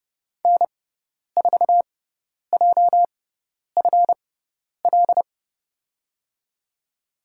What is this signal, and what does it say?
Morse code, heard as a keyed tone, about 30 wpm, spaced as for 10 wpm, 700 Hz. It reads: N4JFL
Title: USA Call Signs - 6x Character Spacing - 30wpm